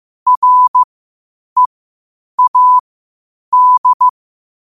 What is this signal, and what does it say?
Morse code, heard as a keyed tone, about 15 wpm, spaced as for 8 wpm, 1000 Hz. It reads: READ
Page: New quotes every day in morse code at 8 Words per minute.